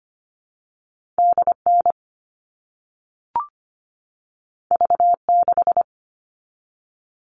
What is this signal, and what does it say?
Morse code, heard as a keyed tone, about 25 wpm, 700 Hz. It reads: DN V6
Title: All Cut Numbers - 2-Digit 25wpm